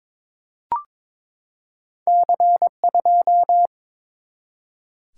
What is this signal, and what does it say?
Morse code, heard as a keyed tone, about 23 wpm, 700 Hz. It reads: C2
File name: practice / 22wpm / Two Letter-Number Combinations - 22wpm